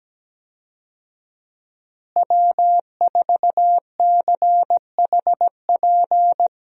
Morse code, heard as a keyed tone, about 17 wpm, 700 Hz. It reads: W4CHP
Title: USA Call Signs - 5 Characters - Repeated 2x - 17wpm